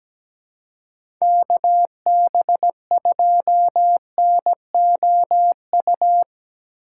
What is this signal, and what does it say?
Morse code, heard as a keyed tone, about 17 wpm, 700 Hz. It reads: KB2NOU
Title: USA Call Signs - 6 Characters - 4x Character Spacing - 17wpm